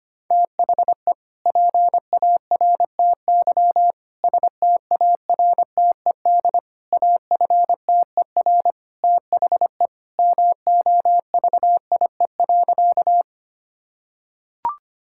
Morse code, heard as a keyed tone, about 25 wpm, 690 Hz. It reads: THE PARTY STARTED AFTER THE MOVIE.
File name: practice / 25wpm / Sentences with 2-Word Phrases - 25wpm